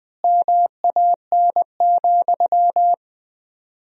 Morse code, heard as a keyed tone, about 20 wpm, 700 Hz. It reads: MAN,
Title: 4x Word-Spacing - Sentences from Top 200 Words - 20wpm